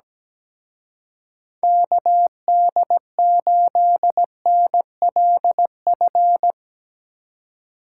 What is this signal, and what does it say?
Morse code, heard as a keyed tone, about 20 wpm, 700 Hz. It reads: KD8NLF
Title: Speed Racing - USA 2x3 Call Signs 17wpm